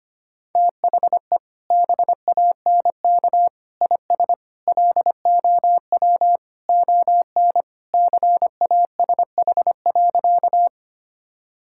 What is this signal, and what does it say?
Morse code, heard as a keyed tone, about 25 wpm, 700 Hz. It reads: THE BANK IS LOW ON CASH.
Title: Sentences from Top 1000 Words - Review Part 5 - 25wpm